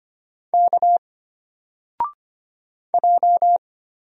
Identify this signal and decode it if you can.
Morse code, heard as a keyed tone, about 25 wpm, 700 Hz. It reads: K J
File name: Single-Letter-Number-25wpm